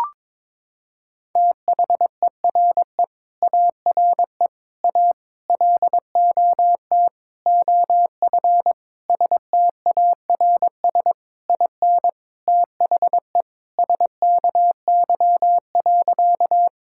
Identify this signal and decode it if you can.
Morse code, heard as a keyed tone, about 22 wpm, 700 Hz. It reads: THERE ARE A LOT OF STARS IN THE SKY.
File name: practice / 22wpm / Sentences with 4-Word Phrases - 22wpm